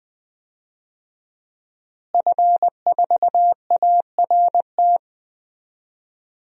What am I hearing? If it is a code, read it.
Morse code, heard as a keyed tone, about 20 wpm, 700 Hz. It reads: F4ART
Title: DXCC-100 Call Signs - 5 Characters - Repeated 2x - 20wpm